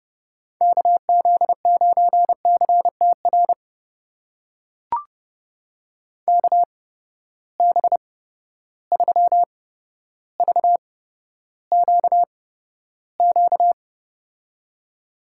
Morse code, heard as a keyed tone, about 30 wpm, spaced as for 13 wpm, 700 Hz. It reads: KZ9CTR KB3VQQ
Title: USA Call Signs - 6 Characters - 8x Character Spacing - 30wpm